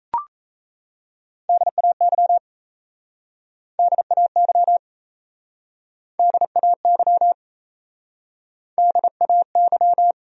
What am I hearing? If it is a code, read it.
Morse code, heard as a keyed tone, about 37 wpm, 700 Hz. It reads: DAY DAY DAY DAY